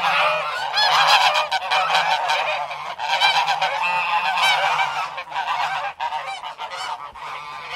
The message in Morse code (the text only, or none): none